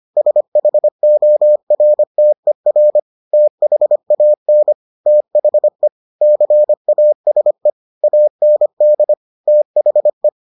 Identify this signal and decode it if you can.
Morse code, heard as a keyed tone, about 25 wpm, 600 Hz. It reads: SHORTER THAN THE CASE AND THE